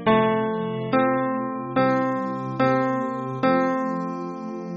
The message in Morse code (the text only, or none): none